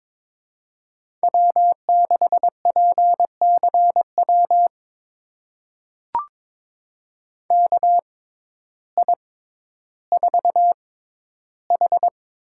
Morse code, heard as a keyed tone, about 22 wpm, spaced as for 12 wpm, 700 Hz. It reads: W6PCW KI4H
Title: USA Call Signs - 6x Character Spacing - 22wpm